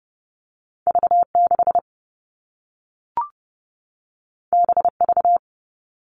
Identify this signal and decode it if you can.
Morse code, heard as a keyed tone, about 30 wpm, 700 Hz. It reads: V6 BV
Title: All Cut Numbers - 2-Digit 30wpm